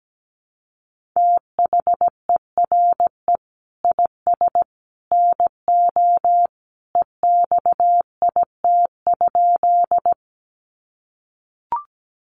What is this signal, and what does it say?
Morse code, heard as a keyed tone, about 17 wpm, 700 Hz. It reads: THERE IS NO EXIT?
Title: Sentences from Top 1100 Words 17wpm